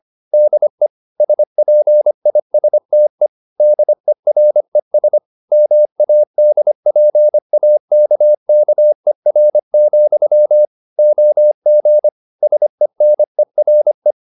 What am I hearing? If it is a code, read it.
Morse code, heard as a keyed tone, about 25 wpm, 600 Hz. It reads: DE SPISTE DERES MADPAKKER, OG SENERE
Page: Ferie 25 wpm